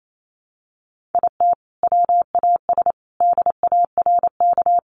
Morse code, heard as a keyed tone, about 28 wpm, 700 Hz. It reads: IT WAS DARK
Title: Sentences from Top 1400 Words 28wpm